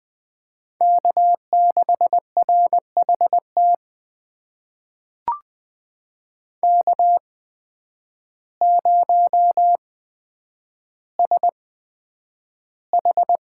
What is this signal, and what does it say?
Morse code, heard as a keyed tone, about 20 wpm, spaced as for 9 wpm, 700 Hz. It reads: K6RHT K0SH
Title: USA Call Signs - 8x Character Spacing - 20wpm